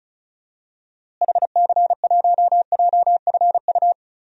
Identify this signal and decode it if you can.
Morse code, heard as a keyed tone, about 36 wpm, 700 Hz. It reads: HC1JFU
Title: DXCC-100 Call Signs - 6 Characters - Repeated 3x - 35wpm